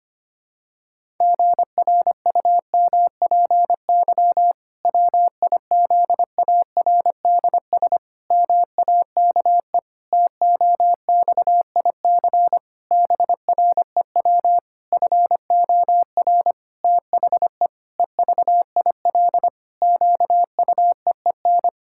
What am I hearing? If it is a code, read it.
Morse code, heard as a keyed tone, about 25 wpm, 700 Hz. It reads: GRUMPY WIZARDS MAKE TOXIC BREW FOR THE EVIL QUEEN
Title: Pangram Challenge v2 - 25wpm